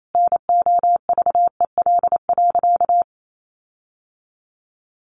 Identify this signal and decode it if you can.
Morse code, heard as a keyed tone, about 28 wpm, 700 Hz. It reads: NOVEL.
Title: Sentences from Top 1300 Words 28wpm